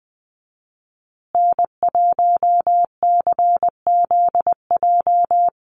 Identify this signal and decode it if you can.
Morse code, heard as a keyed tone, about 20 wpm, 700 Hz. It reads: N1CZJ
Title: USA Call Signs Repeated 3x 20wpm